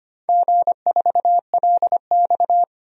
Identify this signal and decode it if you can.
Morse code, heard as a keyed tone, about 25 wpm, 700 Hz. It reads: G4LX
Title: DXCC-100 Call Signs - 4 Characters - 25wpm